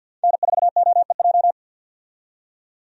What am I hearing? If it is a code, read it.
Morse code, heard as a keyed tone, about 51 wpm, 700 Hz. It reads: N4OEJ
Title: USA Call Signs - 5 Characters - 50wpm